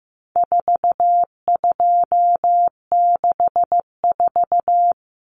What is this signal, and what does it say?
Morse code, heard as a keyed tone, about 15 wpm, 700 Hz. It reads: 4264
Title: Numbers - 4-Digit 15wpm